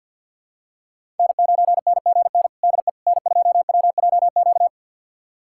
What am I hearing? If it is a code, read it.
Morse code, heard as a keyed tone, about 53 wpm, 700 Hz. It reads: N9NGN DE N2WJQ
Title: USA Callsign DE Callsign 50wpm